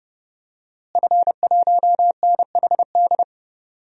Morse code, heard as a keyed tone, about 30 wpm, 700 Hz. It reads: F1NHD
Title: DXCC-100 Call Signs - 5 Characters - Repeated 2x - 30wpm